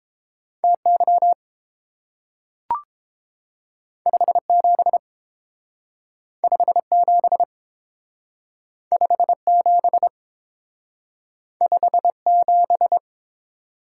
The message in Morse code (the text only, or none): TY 57 57 57 57